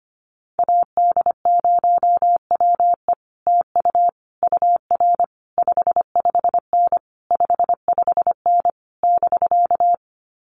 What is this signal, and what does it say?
Morse code, heard as a keyed tone, about 25 wpm, 700 Hz. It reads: AD0WE TU UR 55N 55N <BK>
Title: Common QSO Phrases Encore 25wpm